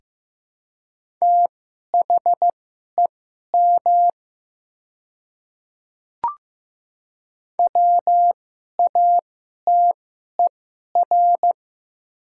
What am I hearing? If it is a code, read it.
Morse code, heard as a keyed tone, about 15 wpm, spaced as for 11 wpm, 700 Hz. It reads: THEM WATER
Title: 2x Character Spacing - Top 100 Words - 15wpm